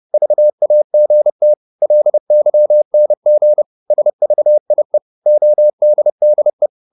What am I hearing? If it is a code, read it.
Morse code, heard as a keyed tone, about 30 wpm, 600 Hz. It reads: VAGT LYNG SVIE ODDE
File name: Korte ord DK 30wpm